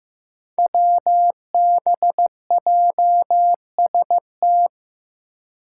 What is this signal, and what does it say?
Morse code, heard as a keyed tone, about 15 wpm, 700 Hz. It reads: WBJST
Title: Five Challenging Letter-Number Combinations 15wpm